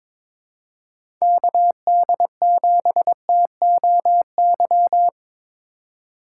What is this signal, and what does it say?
Morse code, heard as a keyed tone, about 22 wpm, 700 Hz. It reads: KD7TOY
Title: USA Call Signs - 4x Character Spacing - 22wpm